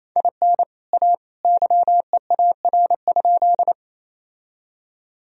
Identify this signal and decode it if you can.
Morse code, heard as a keyed tone, about 27 wpm, 700 Hz. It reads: IN A YEAR?
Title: Sentences from Top 1000 Words - Review Part 1 - 28wpm